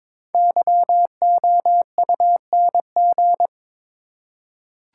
Morse code, heard as a keyed tone, about 22 wpm, 700 Hz. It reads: YOUNG
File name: 5 Letter Words - First 3 Letters Given - 22wpm